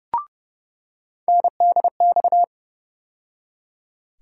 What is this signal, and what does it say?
Morse code, heard as a keyed tone, about 30 wpm, 700 Hz. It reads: NDX
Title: USA Call Sign Postfixes - 3 Letters 30wpm